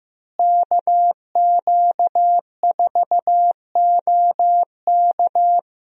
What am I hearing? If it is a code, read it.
Morse code, heard as a keyed tone, about 15 wpm, 700 Hz. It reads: KQ4OK
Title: USA Call Signs - 5 Characters - Repeated 3x - 15wpm